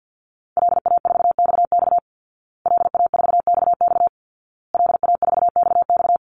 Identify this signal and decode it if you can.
Morse code, heard as a keyed tone, about 50 wpm, 700 Hz. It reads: LA4XX LA4XX LA4XX
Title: Call Signs Repeated 2x - 50wpm